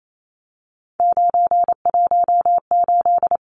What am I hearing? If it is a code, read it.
Morse code, heard as a keyed tone, about 28 wpm, 700 Hz. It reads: 918
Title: Numbers - 3-Digit 28wpm